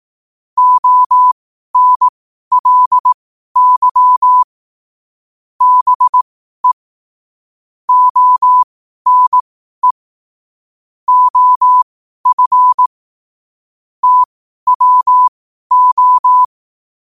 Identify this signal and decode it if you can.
Morse code, heard as a keyed tone, about 18 wpm, spaced as for 12 wpm, 1000 Hz. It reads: ONLY BE ONE OF TWO
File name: qotdfiles / 12wpm